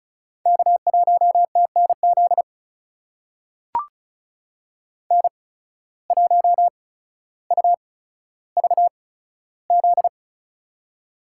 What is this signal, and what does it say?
Morse code, heard as a keyed tone, about 36 wpm, spaced as for 15 wpm, 700 Hz. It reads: K1TNZ N1UVZ
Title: USA Call Signs - 5 Characters - 8x Character Spacing - 35wpm